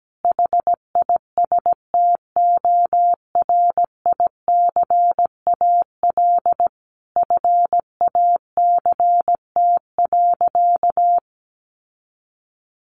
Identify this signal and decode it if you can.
Morse code, heard as a keyed tone, about 17 wpm, 700 Hz. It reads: HISTORICAL FACT.
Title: Sentences from Top 600 Words 17wpm